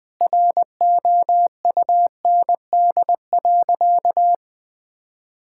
Morse code, heard as a keyed tone, about 20 wpm, 700 Hz. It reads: ROUND.
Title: Sentences with 2-Word Phrases - 20wpm